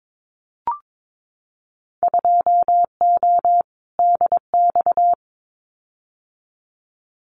Morse code, heard as a keyed tone, about 22 wpm, 700 Hz. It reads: 2O DX
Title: Winter Field Day Exchanges Encore 22wpm